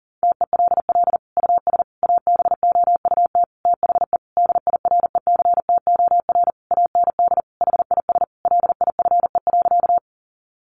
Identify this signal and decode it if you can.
Morse code, heard as a keyed tone, about 40 wpm, 700 Hz. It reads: TELL US ABOUT THE DIRECTOR AND HIS LIFE.
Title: Sentences from Top 1300 Words 40wpm